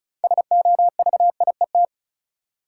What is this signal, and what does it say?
Morse code, heard as a keyed tone, about 34 wpm, 700 Hz. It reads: SOVIET
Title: Sets of 5 Words Encore - New Word Drill - 35wpm